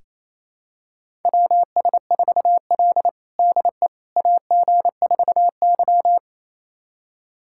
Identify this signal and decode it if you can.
Morse code, heard as a keyed tone, about 27 wpm, 700 Hz. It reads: WS4L DE AG4Y
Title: USA Callsign DE Callsign - 4 Characters - 28wpm